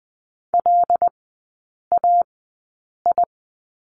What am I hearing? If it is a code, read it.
Morse code, heard as a keyed tone, about 20 wpm, spaced as for 8 wpm, 700 Hz. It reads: LAI
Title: 121 Lesson 20e - Number 3 - Three Characters 20wpm